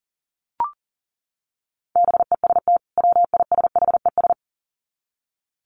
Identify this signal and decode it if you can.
Morse code, heard as a keyed tone, about 40 wpm, 700 Hz. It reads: BEST WISHES
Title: Common QSO Phrases 40wpm